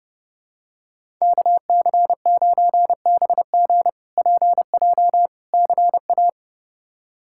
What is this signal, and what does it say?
Morse code, heard as a keyed tone, about 30 wpm, 700 Hz. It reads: KC9BG PJ CA
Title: K1USN SST QSO Practice - 30wpm